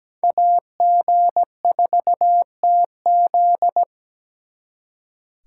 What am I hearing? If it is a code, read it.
Morse code, heard as a keyed tone, about 20 wpm, 700 Hz. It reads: AG4TZ
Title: Speed Racing - USA 2x2 Call Signs 17wpm